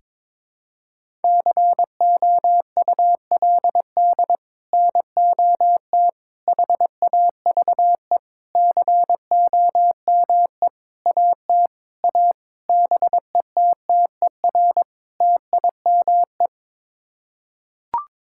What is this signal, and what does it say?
Morse code, heard as a keyed tone, about 23 wpm, 700 Hz. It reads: COULD NOT HAVE COME AT A BETTER TIME
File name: practice / 22wpm / Sets of 8 Words - 22wpm